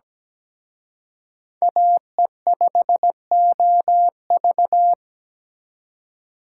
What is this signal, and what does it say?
Morse code, heard as a keyed tone, about 20 wpm, 700 Hz. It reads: AE5OV
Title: Speed Racing - USA 2x2 Call Signs 17wpm